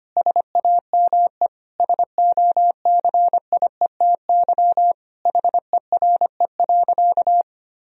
Morse code, heard as a keyed tone, about 25 wpm, 700 Hz. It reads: SAME SOCIETY HERE.
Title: Sentences from Top 1000 Words - Review Part 4 - 25wpm